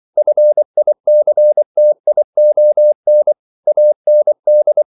Morse code, heard as a keyed tone, about 24 wpm, 600 Hz. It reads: FICTION AND
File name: CW / Practice / EnglishQuotations0424